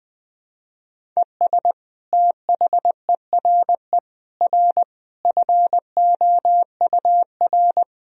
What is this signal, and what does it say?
Morse code, heard as a keyed tone, about 21 wpm, 700 Hz. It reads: ES THERE R FOUR
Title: Sentences from Top 300 Words and Abbreviations 20wpm